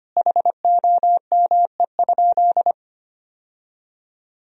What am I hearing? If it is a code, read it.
Morse code, heard as a keyed tone, about 25 wpm, 700 Hz. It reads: HOME?
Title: Sentences from Top 1000 Words - Review Part 1 - 25wpm